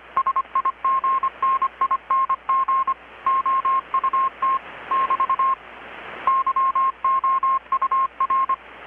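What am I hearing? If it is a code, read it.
Morse code, heard as a keyed tone, about 25 wpm, 1070 Hz. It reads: SIGNING OUT <BT> YOUR
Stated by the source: Morse from KPH's Night of Nights XX maritime radiotelegraphy event, received in Chiapas, Mexico about 3,800 km away on a Kenwood R-600 with a grounded 10 m long-wire antenna.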